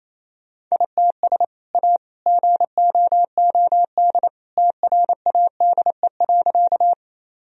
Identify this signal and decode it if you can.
Morse code, heard as a keyed tone, about 28 wpm, 700 Hz. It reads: ITS A GOOD TRADE.
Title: Sentences from Top 1000 Words - Review Part 4 - 28wpm